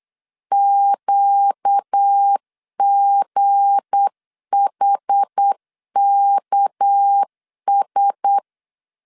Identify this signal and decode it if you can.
Morse code, heard as a keyed tone, about 8 wpm, 800 Hz. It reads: QGHKS
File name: LEZIONI / 9 286 8wpm 3425